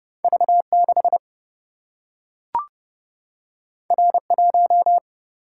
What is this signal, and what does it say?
Morse code, heard as a keyed tone, about 30 wpm, 700 Hz. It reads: V 6 R 1
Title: Single-Letter Single-Number 30wpm